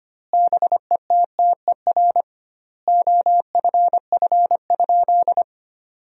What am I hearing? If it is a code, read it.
Morse code, heard as a keyed tone, about 25 wpm, 700 Hz. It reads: BETTER OFF?
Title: 2x Word-Spacing - Sentences from Top 500 Words - 25wpm